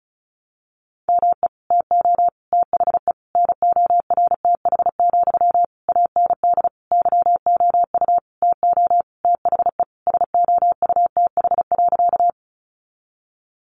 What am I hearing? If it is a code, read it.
Morse code, heard as a keyed tone, about 34 wpm, 700 Hz. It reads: ME TO THE NORTH, AND YOU TO THE SOUTH.
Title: Sentences from Top 400 Words 35wpm